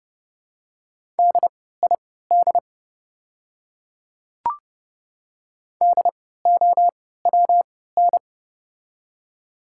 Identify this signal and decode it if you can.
Morse code, heard as a keyed tone, about 30 wpm, spaced as for 17 wpm, 700 Hz. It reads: DID DOWN
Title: 3x Character Spacing - Top 300 Words - 30wpm